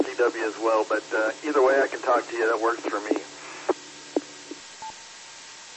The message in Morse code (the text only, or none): none